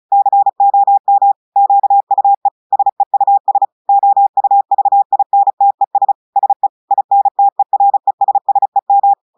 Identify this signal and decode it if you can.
Morse code, heard as a keyed tone, about 36 wpm, 800 Hz. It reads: COM QUE SEUS OUVINTES SE INTERESSEM
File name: portugais texte en clair 2 vitesse 35 mots minutes